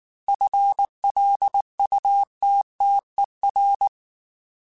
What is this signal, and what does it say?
Morse code, heard as a keyed tone, about 19 wpm, 800 Hz. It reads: FLUTTER
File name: FL Runs / FL Runs mp3s / FL Runs @18